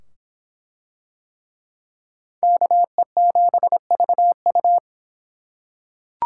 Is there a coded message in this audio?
Yes: Morse code from 2.4 to 4.8 s, heard as a keyed tone, about 25 wpm, 700 Hz. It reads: KE7VU